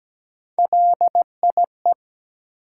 Morse code, heard as a keyed tone, about 17 wpm, 700 Hz. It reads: LIE
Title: Sentences from Top 1600 Words - New Word Drill - 17wpm